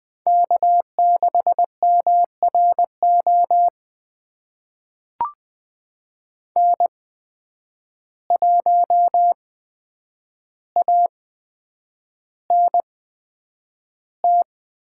Morse code, heard as a keyed tone, about 20 wpm, spaced as for 9 wpm, 700 Hz. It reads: K6MRO N1ANT
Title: USA Call Signs - 5 Characters - 8x Character Spacing - 20wpm